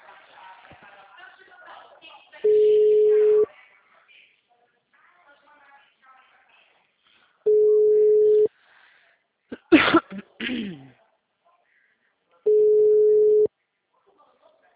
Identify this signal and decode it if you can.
Ringback tone (Europe, 425 Hz), with speech.